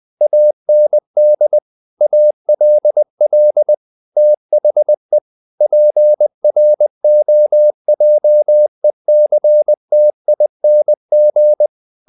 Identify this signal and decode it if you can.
Morse code, heard as a keyed tone, about 20 wpm, 600 Hz. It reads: AND ALL THE PROJECTING
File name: the old man and the sea / the old man and the sea part 1 20wpm